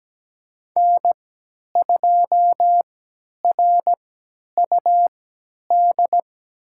Morse code, heard as a keyed tone, about 17 wpm, spaced as for 10 wpm, 700 Hz. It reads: N2RUD
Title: USA Call Signs - 3x Character Spacing - 17wpm